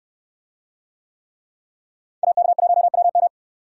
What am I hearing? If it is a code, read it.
Morse code, heard as a keyed tone, about 69 wpm, 700 Hz. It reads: AK1WM